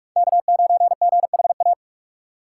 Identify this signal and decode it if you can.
Morse code, heard as a keyed tone, about 46 wpm, 700 Hz. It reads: K9GHA
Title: USA Call Signs - 3x Character Spacing - 45wpm